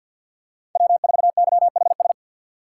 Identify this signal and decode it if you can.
Morse code, heard as a keyed tone, about 47 wpm, 690 Hz. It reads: W4YHS
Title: USA Call Signs - 5 Characters - 4x Character Spacing - 50wpm